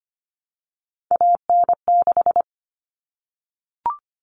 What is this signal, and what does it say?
Morse code, heard as a keyed tone, about 25 wpm, 700 Hz. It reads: AN6
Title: Common Cut Numbers - 3-Digit 25wpm